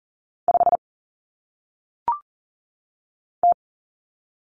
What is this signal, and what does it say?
Morse code, heard as a keyed tone, about 39 wpm, 700 Hz. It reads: 5 T
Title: Common Cut Numbers - 1-Digit 40wpm